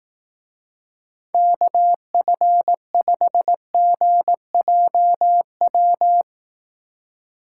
Morse code, heard as a keyed tone, about 18 wpm, 700 Hz. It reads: KF5GJW
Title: Speed Racing - USA 2x3 Call Signs 15wpm